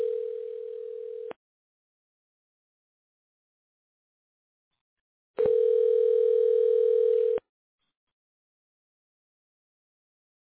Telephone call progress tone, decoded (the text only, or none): ringback tone